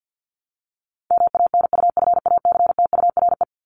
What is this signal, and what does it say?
Morse code, heard as a keyed tone, about 48 wpm, 700 Hz. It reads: MANUFACTURE
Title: ING Suffix - Top 100 Words with Base Words 50wpm